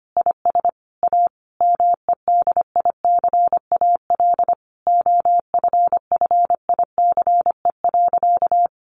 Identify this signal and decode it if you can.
Morse code, heard as a keyed tone, about 25 wpm, 700 Hz. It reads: IS A MEDICAL OFFICE.
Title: Additional Easy Phrases 25wpm